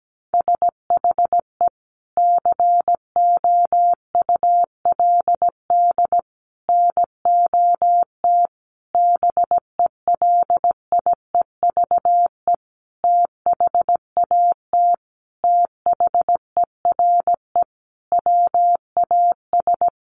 Morse code, heard as a keyed tone, about 17 wpm, 700 Hz. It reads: SHE COULD NOT BELIEVE THAT THERE WAS
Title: Sentences from Top 1000 Words 17wpm